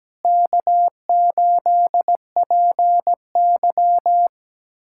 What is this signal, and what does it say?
Morse code, heard as a keyed tone, about 17 wpm, 700 Hz. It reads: K8PY
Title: USA Callsign DE Callsign - 4 Characters - 17wpm